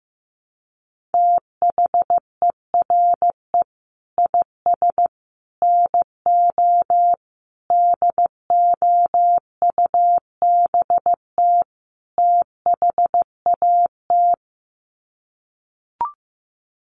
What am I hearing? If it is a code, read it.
Morse code, heard as a keyed tone, about 15 wpm, 700 Hz. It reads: THERE IS NO DOUBT THAT
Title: Sets of 5 Words 15wpm